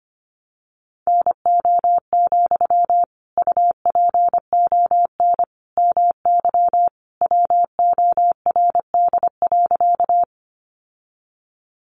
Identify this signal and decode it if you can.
Morse code, heard as a keyed tone, about 25 wpm, 700 Hz. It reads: NO, UPON MY WORD.
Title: Sentences from Top 1100 Words 25wpm